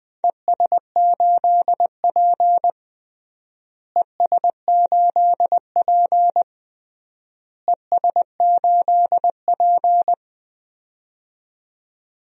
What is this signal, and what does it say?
Morse code, heard as a keyed tone, about 20 wpm, 700 Hz. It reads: ES8P ES8P ES8P
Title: DXCC-100 Call Signs - 4 Characters - Repeated 3x - 20wpm